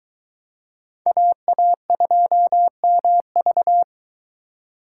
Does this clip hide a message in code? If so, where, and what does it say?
Yes: Morse code from 1.1 to 3.8 s, heard as a keyed tone, about 23 wpm, 700 Hz. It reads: AA2MV